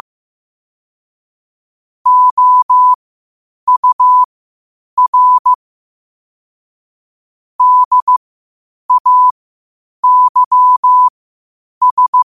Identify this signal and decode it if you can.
Morse code, heard as a keyed tone, about 15 wpm, spaced as for 8 wpm, 1000 Hz. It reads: OUR DAYS
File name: qotdfiles / 8wpm